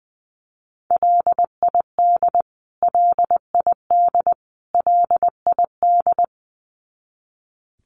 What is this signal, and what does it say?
Morse code, heard as a keyed tone, about 20 wpm, 700 Hz. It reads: LID LID LID
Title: Abbreviations Repeated 3x - 20wpm